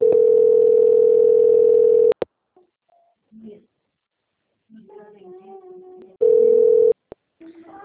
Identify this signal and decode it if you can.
Ringback tone (North America), 440 + 480 Hz, with speech.
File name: out- - - - - .94894